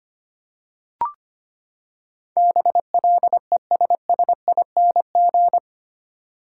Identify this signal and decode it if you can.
Morse code, heard as a keyed tone, about 25 wpm, 700 Hz. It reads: BLESSING
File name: Eight Letter Words - 25wpm